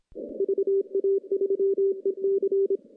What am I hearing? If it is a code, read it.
Morse code, heard as a keyed tone, about 24 wpm, 410 Hz. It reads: VA3EC
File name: contests / CW-SS-k5tr- -0200